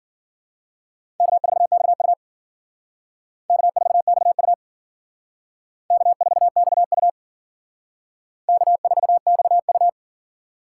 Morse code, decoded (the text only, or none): K4XU K4XU K4XU K4XU